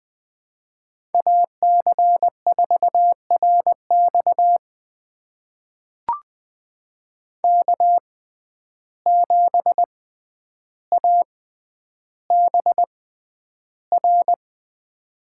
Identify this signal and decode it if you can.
Morse code, heard as a keyed tone, about 20 wpm, spaced as for 10 wpm, 700 Hz. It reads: AC4RX K7ABR
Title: USA Call Signs - 5 Characters - 6x Character Spacing - 20wpm